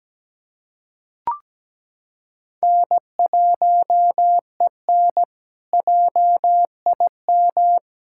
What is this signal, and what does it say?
Morse code, heard as a keyed tone, about 17 wpm, 700 Hz. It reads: N1EN JIM
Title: K1USN SST QSO Practice - 17wpm